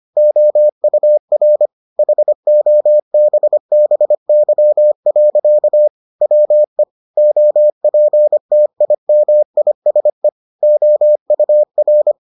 Morse code, heard as a keyed tone, about 25 wpm, 600 Hz. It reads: OUR HOBBY. WE OPTIMISE OUR